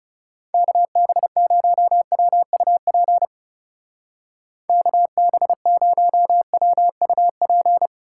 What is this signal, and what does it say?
Morse code, heard as a keyed tone, about 35 wpm, 700 Hz. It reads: KB0WUP KB0WUP
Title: Speed Racing - USA 2x3 Call Signs 30wpm